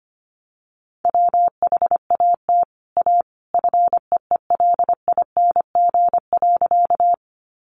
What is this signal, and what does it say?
Morse code, heard as a keyed tone, about 25 wpm, 700 Hz. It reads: WHAT A FEELING.
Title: Sentences from Top 1200 Words 25wpm